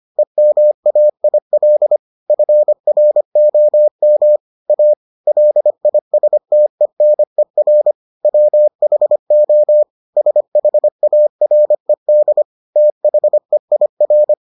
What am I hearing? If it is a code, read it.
Morse code, heard as a keyed tone, about 25 wpm, 600 Hz. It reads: EMAIL FROM A LISTENER WHO SHARED THEIR